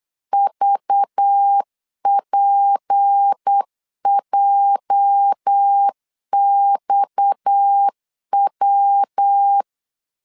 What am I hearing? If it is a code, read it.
Morse code, heard as a keyed tone, about 8 wpm, 800 Hz. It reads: VPJXW
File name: CW / DET / 1 9 1500C S C 8wpm 5653